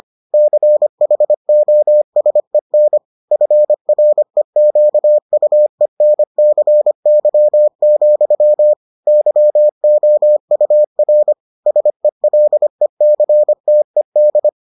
Morse code, heard as a keyed tone, about 25 wpm, 600 Hz. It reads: CHOSEN FREQUENCY, YOUR SELECTED